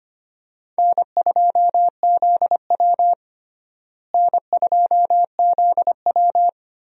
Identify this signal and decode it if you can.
Morse code, heard as a keyed tone, about 25 wpm, 700 Hz. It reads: N2ZW N2ZW
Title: USA Call Signs - 4 Characters - Repeated 2x - 25wpm